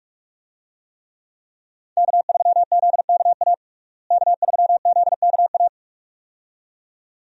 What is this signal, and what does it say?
Morse code, heard as a keyed tone, about 46 wpm, 700 Hz. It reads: K3ZKA K3ZKA
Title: USA Call Signs - 5 Characters - Repeated 2x - 45wpm